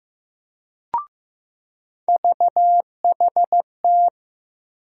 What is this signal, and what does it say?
Morse code, heard as a keyed tone, about 15 wpm, 700 Hz. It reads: VHT
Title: USA Call Sign Postfixes - 3 Letters 15wpm